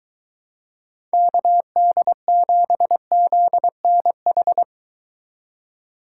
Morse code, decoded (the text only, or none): KD7ZNH